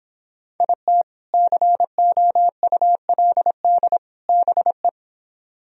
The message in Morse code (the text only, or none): IT COULD BE